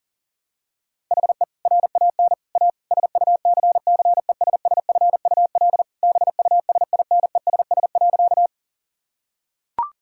Morse code, heard as a keyed tone, about 40 wpm, 700 Hz. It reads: HE RAN A SUCCESSFUL BUSINESS.
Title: Sentences from Top 1000 Words - Review Part 6 - 40wpm